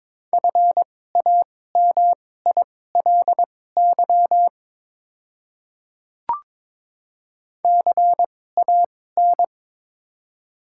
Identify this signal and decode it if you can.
Morse code, heard as a keyed tone, about 22 wpm, spaced as for 16 wpm, 700 Hz. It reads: FAMILY CAN
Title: 2x Character Spacing - Top 500 Words - 22wpm